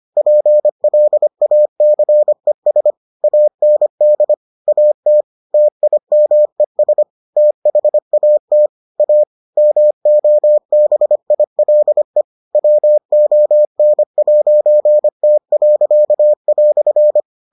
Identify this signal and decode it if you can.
Morse code, heard as a keyed tone, about 25 wpm, 600 Hz. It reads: PLACES AND AT TIMES THAT A MOBILE WON'T."